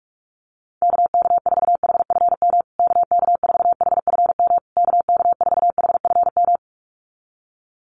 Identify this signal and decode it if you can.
Morse code, heard as a keyed tone, about 44 wpm, 700 Hz. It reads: KK4HFM KK4HFM KK4HFM
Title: USA Call Signs Repeated 3x 45wpm